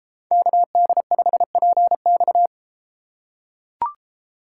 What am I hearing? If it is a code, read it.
Morse code, heard as a keyed tone, about 32 wpm, 700 Hz. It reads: KD5PX